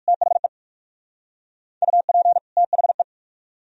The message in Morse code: THE UP THE

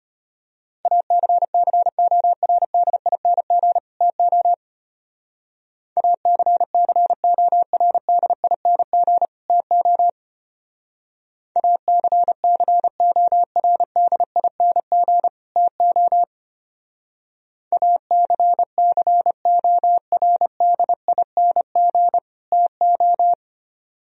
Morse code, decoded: ACCORDING TO ACCORDING TO ACCORDING TO ACCORDING TO

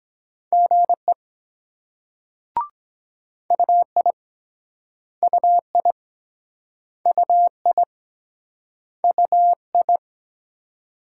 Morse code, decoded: GE UI UI UI UI

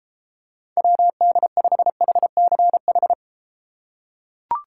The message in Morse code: WD5HCH

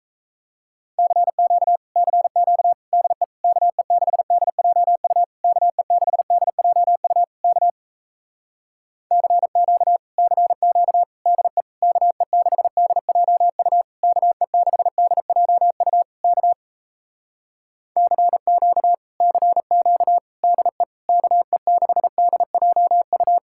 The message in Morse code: CQ CQ DE KE6DJU KE6DJU K CQ CQ DE KE6DJU KE6DJU K CQ CQ DE KE6DJU

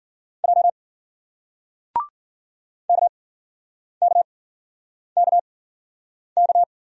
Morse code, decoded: J KKKK

